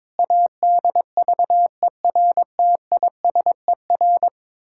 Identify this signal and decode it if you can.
Morse code, heard as a keyed tone, about 22 wpm, 700 Hz. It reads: ADVERTISER
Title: Pangram Challenge v2 - 22wpm